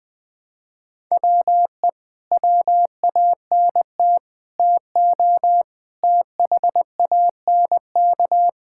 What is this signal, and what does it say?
Morse code, heard as a keyed tone, about 21 wpm, 700 Hz. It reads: WE WANT TO THANK
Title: Sentences with 9-Word Phrases - 20wpm